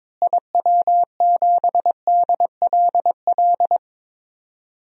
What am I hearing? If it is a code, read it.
Morse code, heard as a keyed tone, about 22 wpm, 700 Hz. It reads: IW7DLL
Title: DXCC-100 Call Signs - 6 Characters - Repeated 3x - 22wpm